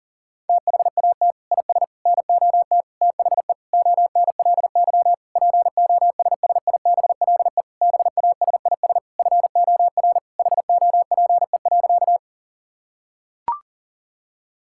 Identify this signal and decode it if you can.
Morse code, heard as a keyed tone, about 40 wpm, 700 Hz. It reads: THAT IS NOT THE ONLY POSSIBLE BASIS FOR HOPE.
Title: Sentences from Top 1000 Words - Review Part 5 - 40wpm